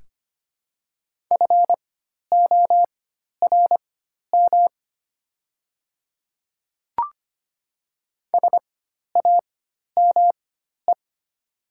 Morse code, heard as a keyed tone, about 25 wpm, spaced as for 12 wpm, 700 Hz. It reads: FORM SAME
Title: 4x Character Spacing - Top 300 Words - 25wpm